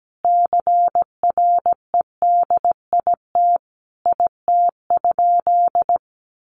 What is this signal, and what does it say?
Morse code, heard as a keyed tone, about 17 wpm, 700 Hz. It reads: CREDIT IT?
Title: Sentences from Top 1000 Words 17wpm